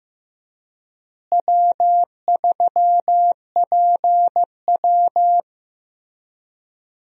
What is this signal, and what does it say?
Morse code, heard as a keyed tone, about 15 wpm, 700 Hz. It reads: W3PW
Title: USA Call Signs - 4 Characters - Repeated 3x - 15wpm